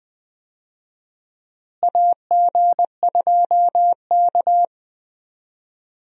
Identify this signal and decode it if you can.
Morse code, heard as a keyed tone, about 20 wpm, 700 Hz. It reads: AG2K
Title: USA Call Signs - 4 Characters - Repeated 3x - 20wpm